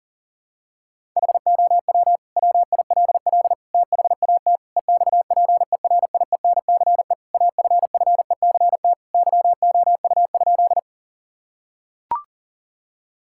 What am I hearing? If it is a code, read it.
Morse code, heard as a keyed tone, about 40 wpm, 700 Hz. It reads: HOW WILL THAT EXPERIENCE AFFECT YOU?
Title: Sentences from Top 1000 Words - Review Part 6 - 40wpm